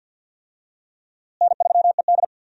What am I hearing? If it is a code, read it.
Morse code, heard as a keyed tone, about 51 wpm, 700 Hz. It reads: N3ED
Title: USA Call Signs - 4 Characters - Repeated 3x - 50wpm